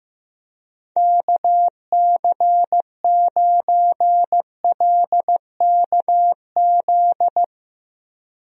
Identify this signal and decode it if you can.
Morse code, heard as a keyed tone, about 16 wpm, 700 Hz. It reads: KC9LKZ
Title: USA Call Signs - 6 Characters - Repeated 2x - 15wpm